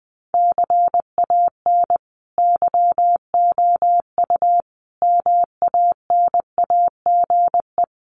Morse code, heard as a keyed tone, about 20 wpm, 700 Hz. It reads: CAN YOU MANAGE
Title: Sentences from Top 1400 Words 20wpm